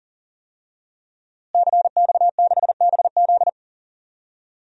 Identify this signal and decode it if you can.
Morse code, heard as a keyed tone, about 40 wpm, 700 Hz. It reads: CX6BZ
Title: DXCC-100 Call Signs - 5 Characters - Repeated 2x - 40wpm